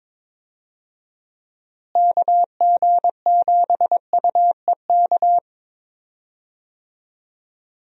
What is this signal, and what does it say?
Morse code, heard as a keyed tone, about 22 wpm, 700 Hz. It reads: KG7UEK
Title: USA Call Signs - 6 Characters - Repeated 3x - 22wpm